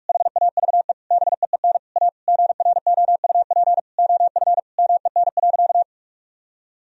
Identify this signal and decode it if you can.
Morse code, heard as a keyed tone, about 41 wpm, 700 Hz. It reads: HAVE BEEN A GROUP OF MEN.
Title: Sentences from Top 1000 Words - Review Part 1 - 45wpm